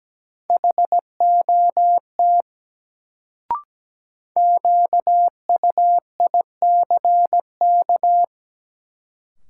Morse code, heard as a keyed tone, about 17 wpm, 700 Hz. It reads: HOT QUICK
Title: Sentences from Top 500 Words - New Word Drill - 17wpm